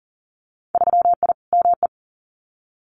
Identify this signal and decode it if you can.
Morse code, heard as a keyed tone, about 38 wpm, 700 Hz. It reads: 3I ME
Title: Winter Field Day Exchanges 40wpm